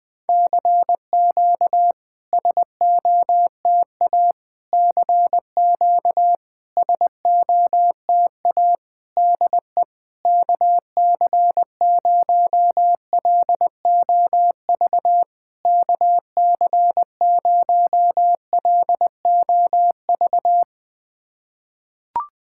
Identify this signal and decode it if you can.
Morse code, heard as a keyed tone, about 20 wpm, 690 Hz. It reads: CQ SOTA CQ SOTA DE KC0LOV KC0LOV
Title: SOTA QSO Practice 20wpm